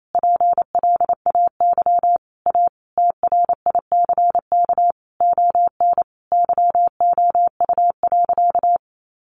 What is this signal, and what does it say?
Morse code, heard as a keyed tone, about 28 wpm, 700 Hz. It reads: PLAY A TRICK ON YOU.
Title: Sentences from Top 1500 Words 28wpm